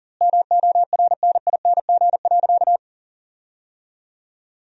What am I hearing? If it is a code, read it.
Morse code, heard as a keyed tone, about 40 wpm, 700 Hz. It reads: MORNING.
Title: 2x Word-Spacing - Sentences from Top 400 Words - 40wpm